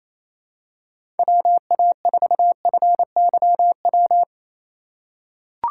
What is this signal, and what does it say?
Morse code, heard as a keyed tone, about 27 wpm, 700 Hz. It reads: WA4FYW
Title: USA Call Signs - 6 Characters - 2x Character Spacing - 28wpm